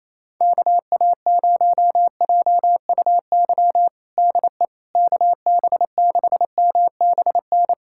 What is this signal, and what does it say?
Morse code, heard as a keyed tone, about 28 wpm, 700 Hz. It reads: KA0JUY DE KB6MBN
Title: USA Callsign DE Callsign 28wpm